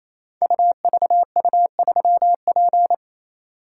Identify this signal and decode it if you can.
Morse code, heard as a keyed tone, about 27 wpm, 700 Hz. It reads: UVU3P
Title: Code Groups - 5 Letter:Numbers - 28wpm